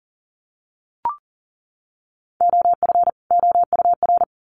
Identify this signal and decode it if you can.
Morse code, heard as a keyed tone, about 40 wpm, 700 Hz. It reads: OF OUR
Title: Sets of 2 Words 40wpm